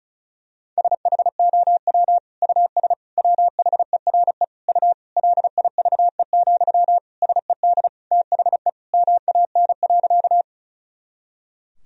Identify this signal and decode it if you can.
Morse code, heard as a keyed tone, about 36 wpm, 700 Hz. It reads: SHOW US WHERE U LIVE, SED THE MAN.
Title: Sentences from Top 200 Words and Abbreviations 35wpm